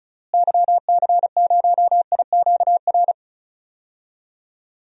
Morse code, heard as a keyed tone, about 34 wpm, 700 Hz. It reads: YC0IQR
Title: DXCC-100 Call Signs - 6 Characters - Repeated 2x - 35wpm